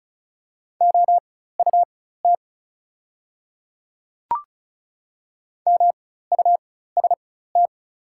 Morse code, heard as a keyed tone, about 34 wpm, spaced as for 16 wpm, 700 Hz. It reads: OUT MUST
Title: 4x Character Spacing - Top 200 Words - 35wpm